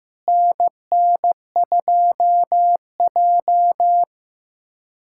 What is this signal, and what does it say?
Morse code, heard as a keyed tone, about 15 wpm, 700 Hz. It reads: NN2J
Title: USA Callsign DE Callsign - 4 Characters - 15wpm